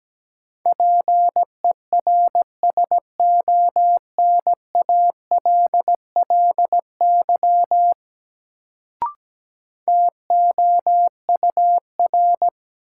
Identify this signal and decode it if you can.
Morse code, heard as a keyed tone, about 17 wpm, 700 Hz. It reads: PERSONALLY TOUR
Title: Sentences from Top 1400 Words - New Word Drill - 17wpm